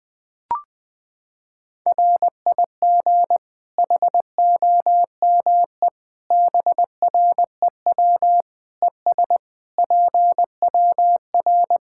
Morse code, heard as a keyed tone, about 20 wpm, 700 Hz. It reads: RIG HOME BREW ES PWR
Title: Practice QSO - Exchange 2 - Info - 20wpm